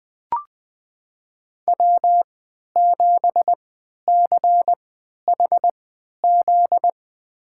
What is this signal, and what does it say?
Morse code, heard as a keyed tone, about 20 wpm, spaced as for 11 wpm, 700 Hz. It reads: W7CHZ
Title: USA Call Signs - 5 Characters - 3x Character Spacing - 20wpm